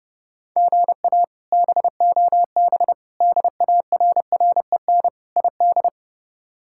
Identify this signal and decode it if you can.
Morse code, heard as a keyed tone, about 30 wpm, 700 Hz. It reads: GA BOB DARREN ID
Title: K1USN SST QSO Practice - 30wpm